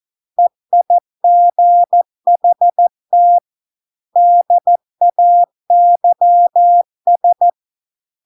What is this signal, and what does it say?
Morse code, heard as a keyed tone, about 14 wpm, 700 Hz. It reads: EIGHT DAYS